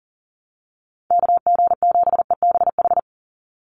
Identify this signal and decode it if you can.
Morse code, heard as a keyed tone, about 40 wpm, 700 Hz. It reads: KG7EBH
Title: USA Call Signs 40wpm